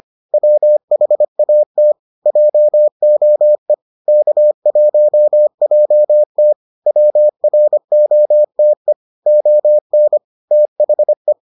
Morse code, heard as a keyed tone, about 25 wpm, 600 Hz. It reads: WHAT JOE K1JT WROTE ON THE